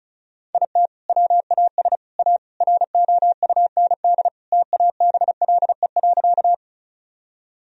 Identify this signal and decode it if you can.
Morse code, heard as a keyed tone, about 34 wpm, 700 Hz. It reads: IT WAS A ROUND TABLE.
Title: Sentences from Top 1000 Words - Review Part 2 - 35wpm